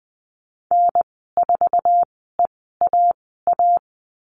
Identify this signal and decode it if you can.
Morse code, heard as a keyed tone, about 19 wpm, spaced as for 14 wpm, 700 Hz. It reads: N4EAA
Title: 069 Lesson 12g - Letter d - Call Signs 20wpm